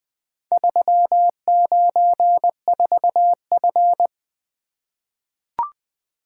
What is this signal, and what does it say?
Morse code, heard as a keyed tone, about 20 wpm, 700 Hz. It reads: 394F
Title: Four Letter-Number Combinations - 20wpm